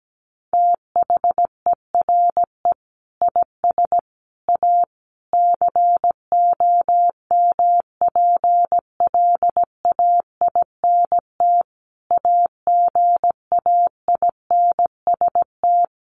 Morse code, heard as a keyed tone, about 17 wpm, 700 Hz. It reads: THERE IS A COMPLAINT AGAINST
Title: Sentences from Top 1400 Words 17wpm